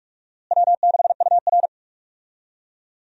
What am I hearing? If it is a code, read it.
Morse code, heard as a keyed tone, about 47 wpm, 700 Hz. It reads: W6UR
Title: USA Call Signs - 4 Characters - Repeated 3x - 45wpm